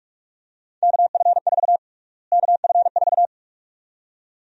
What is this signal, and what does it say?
Morse code, heard as a keyed tone, about 44 wpm, 700 Hz. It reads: KF4 KF4
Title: USA Call Sign Prefixes - Repeated 2x - 45wpm